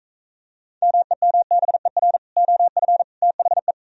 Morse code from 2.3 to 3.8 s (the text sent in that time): OF THE